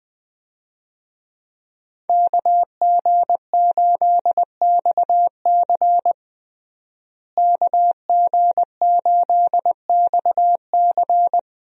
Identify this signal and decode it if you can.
Morse code, heard as a keyed tone, about 20 wpm, 700 Hz. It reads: KG8XC KG8XC
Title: USA Call Signs - 5 Characters - Repeated 3x - 20wpm